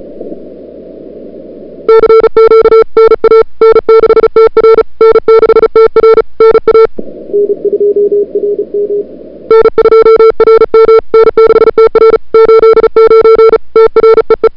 Morse code, heard as a keyed tone, about 34 wpm, 440 Hz. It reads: CQ NA N6TR N6TR NA N2RM N2RM N6TR 89 TREE
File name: n6tr / Fab2007CWSprintHour1